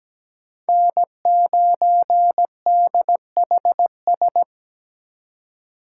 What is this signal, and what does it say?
Morse code, heard as a keyed tone, about 20 wpm, 700 Hz. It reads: N9DHS
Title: Speed Racing - USA Call Signs 17wpm